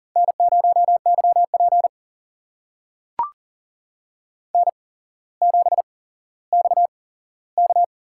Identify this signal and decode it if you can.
Morse code, heard as a keyed tone, about 39 wpm, spaced as for 17 wpm, 700 Hz. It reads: N0YP N7XK
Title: USA Call Signs - 4 Characters - 8x Character Spacing - 40wpm